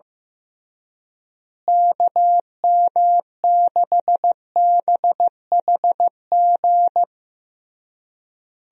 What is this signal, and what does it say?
Morse code, heard as a keyed tone, about 15 wpm, 700 Hz. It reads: KM6BHG
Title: USA Call Signs - 6 Characters - Repeated 3x - 15wpm